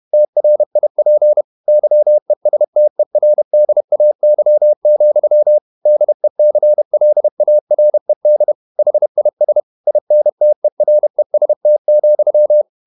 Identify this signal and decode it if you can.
Morse code, heard as a keyed tone, about 31 wpm, 600 Hz. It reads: TRIP YESTERDAY, DECLARED HIS INTEREST,
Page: Never 31wpm | CW med Gnister